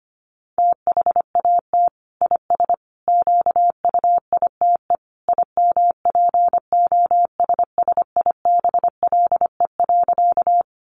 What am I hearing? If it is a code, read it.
Morse code, heard as a keyed tone, about 25 wpm, 700 Hz. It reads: THAT IS QUITE IMPOSSIBLE.
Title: Sentences from Top 1100 Words 25wpm